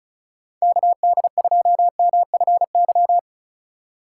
Morse code, heard as a keyed tone, about 36 wpm, 700 Hz. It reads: KD2MFY
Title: USA Call Signs - 6 Characters - 3x Character Spacing - 35wpm